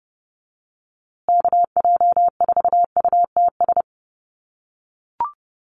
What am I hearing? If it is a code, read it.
Morse code, heard as a keyed tone, about 30 wpm, 700 Hz. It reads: KJ4UTS
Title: USA Call Signs Repeated 3x 30wpm